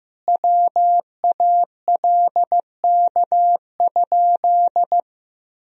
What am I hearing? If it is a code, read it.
Morse code, heard as a keyed tone, about 15 wpm, 700 Hz. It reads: WALK?
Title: Sentences from Top 1000 Words - Review Part 1 - 15wpm